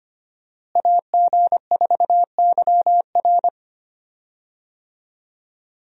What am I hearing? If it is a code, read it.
Morse code, heard as a keyed tone, about 25 wpm, 700 Hz. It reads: AG4YR
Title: USA Call Signs - 5 Characters - Repeated 2x - 25wpm